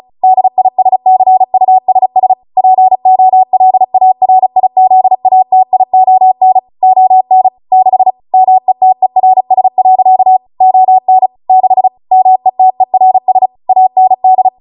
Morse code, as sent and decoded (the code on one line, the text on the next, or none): -.. .. ... -.-. ..- ... ... / .--. --- .-.. .- .-. .. --.. .- - .. --- -. / --- -. / -.... / -- . - . .-. ... .-.-.- / --- -. / -.... / -- . - . .-. ... / .- -. -..
DISCUSS POLARIZATION ON 6 METERS. ON 6 METERS AND